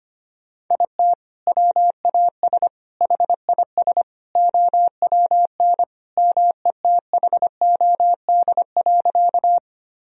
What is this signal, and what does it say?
Morse code, heard as a keyed tone, about 25 wpm, 700 Hz. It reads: IT WAS HIS OWN METHOD.
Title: Sentences from Top 1000 Words - Review Part 3 - 25wpm